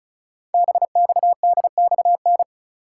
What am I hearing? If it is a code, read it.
Morse code, heard as a keyed tone, about 36 wpm, 700 Hz. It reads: BXDXN
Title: Code Groups - 5 Letter:Numbers - 35wpm